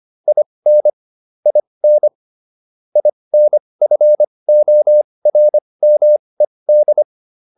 Morse code, heard as a keyed tone, about 25 wpm, spaced as for 20 wpm, 600 Hz. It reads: IN IN INFORMED
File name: IN8 20